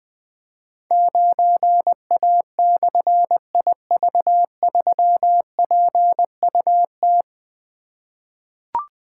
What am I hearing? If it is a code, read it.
Morse code, heard as a keyed tone, about 20 wpm, 700 Hz. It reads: 9A/IV3PUT
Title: Difficult Call Signs 20wpm